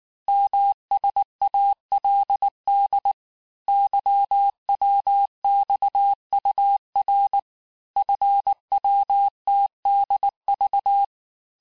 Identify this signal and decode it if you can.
Morse code, heard as a keyed tone, about 19 wpm, 800 Hz. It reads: MSALD YWXUR FWTDV
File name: cours-du-lundi files / cours-1589-texte-chiffre-vitesse-18mpm